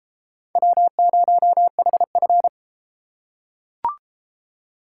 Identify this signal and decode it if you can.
Morse code, heard as a keyed tone, about 33 wpm, 700 Hz. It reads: W0HF